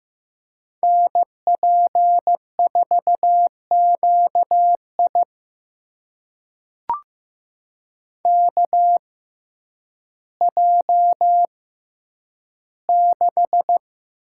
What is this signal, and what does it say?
Morse code, heard as a keyed tone, about 15 wpm, spaced as for 9 wpm, 700 Hz. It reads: NP4QI KJ6
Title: USA Call Signs - 5 Characters - 6x Character Spacing - 15wpm